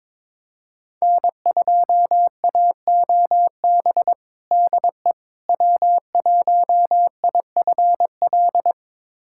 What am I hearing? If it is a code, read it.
Morse code, heard as a keyed tone, about 22 wpm, 700 Hz. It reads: N2AOB DE W1IFL
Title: USA Callsign DE Callsign - 5 Characters - 22wpm